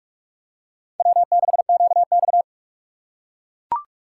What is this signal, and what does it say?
Morse code, heard as a keyed tone, about 41 wpm, 690 Hz. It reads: W6QX
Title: USA Call Signs - 4 Characters - 2x Character Spacing - 45wpm